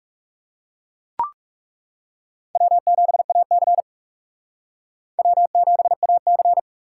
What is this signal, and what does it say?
Morse code, heard as a keyed tone, about 40 wpm, 700 Hz. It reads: W7AC W7AC